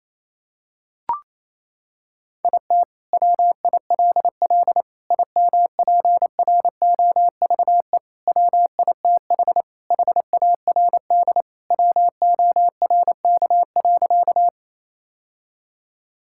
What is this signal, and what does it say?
Morse code, heard as a keyed tone, about 28 wpm, 700 Hz. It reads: IT WILL IMPROVE WITH HARD WORK.
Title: Sentences from Top 1000 Words - Review Part 5 - 28wpm